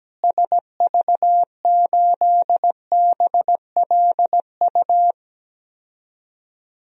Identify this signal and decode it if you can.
Morse code, heard as a keyed tone, about 17 wpm, 700 Hz. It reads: SV8BLU
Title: DXCC-100 Call Signs - 6 Characters - Repeated 2x - 17wpm